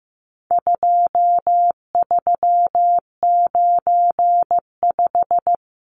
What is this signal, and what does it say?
Morse code, heard as a keyed tone, about 15 wpm, 700 Hz. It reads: 2395
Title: Numbers - 4-Digit 15wpm